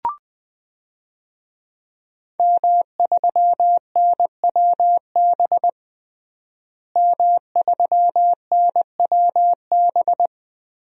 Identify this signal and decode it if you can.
Morse code, heard as a keyed tone, about 21 wpm, 700 Hz. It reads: M3NWB M3NWB
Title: DXCC-100 Call Signs - 5 Characters - Repeated 3x - 20wpm